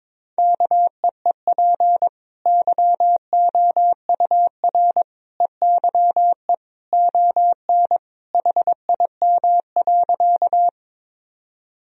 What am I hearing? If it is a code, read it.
Morse code, heard as a keyed tone, about 22 wpm, 700 Hz. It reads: KEEP YOUR EYE ON HIM.
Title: Sentences from Top 1000 Words - Review Part 1 - 22wpm